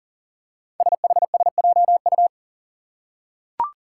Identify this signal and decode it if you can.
Morse code, heard as a keyed tone, about 39 wpm, 700 Hz. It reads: SHSJU
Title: Five Challenging Letter-Number Combinations 40wpm